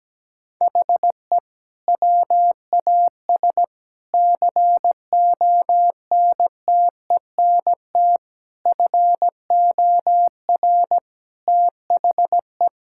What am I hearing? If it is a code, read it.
Morse code, heard as a keyed tone, about 17 wpm, 700 Hz. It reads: HE WAS CONTENT FOR THE
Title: Sentences from Top 1000 Words - Review Part 6 - 17wpm